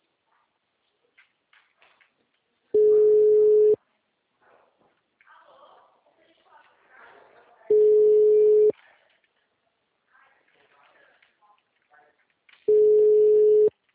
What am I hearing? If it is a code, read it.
Ringback tone (Europe, 425 Hz), with speech.
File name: out- -2011- - - .8580